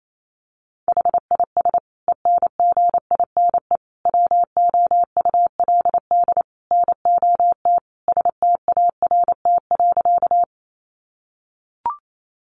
Morse code, heard as a keyed tone, about 28 wpm, 700 Hz. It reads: HIS ENGINE WOULD NOT START.
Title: Sentences from Top 1500 Words 28wpm